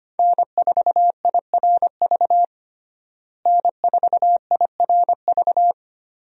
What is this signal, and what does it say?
Morse code, heard as a keyed tone, about 25 wpm, 700 Hz. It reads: N4IRV N4IRV
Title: USA Call Signs - 5 Characters - Repeated 2x - 25wpm